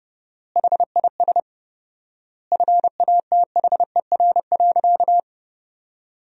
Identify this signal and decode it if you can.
Morse code, heard as a keyed tone, about 30 wpm, 700 Hz. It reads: HIS FATHER.
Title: 4x Word-Spacing - Sentences from Top 500 Words - 30wpm